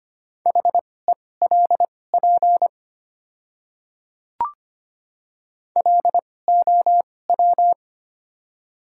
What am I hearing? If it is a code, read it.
Morse code, heard as a keyed tone, about 25 wpm, spaced as for 18 wpm, 700 Hz. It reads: HELP LOW
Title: 2x Character Spacing - Top 300 Words - 25wpm